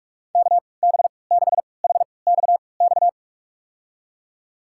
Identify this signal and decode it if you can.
Morse code, heard as a keyed tone, about 41 wpm, spaced as for 25 wpm, 700 Hz. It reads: KB6HXX
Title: USA Call Signs - 6 Characters - 3x Character Spacing - 45wpm